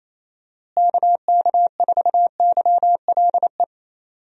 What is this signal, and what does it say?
Morse code, heard as a keyed tone, about 28 wpm, 700 Hz. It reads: KK4YLE
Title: USA Call Signs - 6 Characters - Repeated 2x - 28wpm